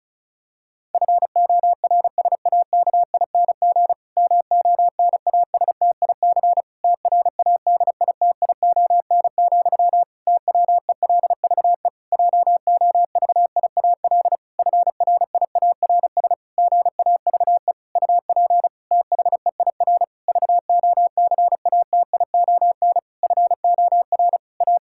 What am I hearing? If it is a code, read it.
Morse code, heard as a keyed tone, about 34 wpm, 700 Hz. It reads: FORSAKING MONASTIC TRADITION, TWELVE JOVIAL FRIARS GAVE UP THEIR VOCATION FOR A
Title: Pangram Challenge v2 - 35wpm